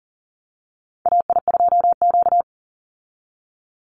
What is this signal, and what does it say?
Morse code, heard as a keyed tone, about 40 wpm, 700 Hz. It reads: AI2Q
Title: Call Signs Repeated 2x - 40wpm